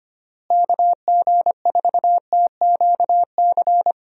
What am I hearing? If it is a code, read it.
Morse code, heard as a keyed tone, about 25 wpm, 700 Hz. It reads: KG4TQC
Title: USA Call Signs - 6 Characters - Repeated 2x - 25wpm